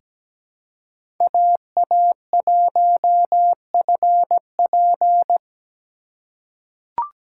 Morse code, heard as a keyed tone, about 17 wpm, 700 Hz. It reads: AA1FP
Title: USA Call Signs - 5 Characters - 4x Character Spacing - 17wpm